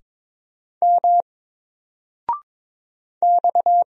Morse code, heard as a keyed tone, about 22 wpm, 700 Hz. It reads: M X
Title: Single Character (Letter:Number:Punctuation:Prosign) Alt 22wpm